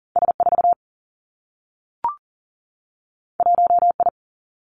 Morse code, heard as a keyed tone, about 39 wpm, 700 Hz. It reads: S4 1I
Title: Two Challenging Letter-Number Combinations 40wpm